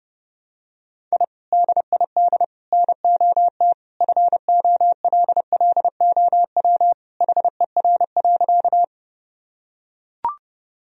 Morse code, heard as a keyed tone, about 32 wpm, 700 Hz. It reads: I DID NOT FOLLOW HER.
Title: Sentences from Top 1000 Words - Review Part 1 - 30wpm